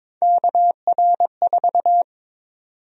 Morse code, heard as a keyed tone, about 22 wpm, 700 Hz. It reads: KR4
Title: USA Call Sign Prefixes - 22wpm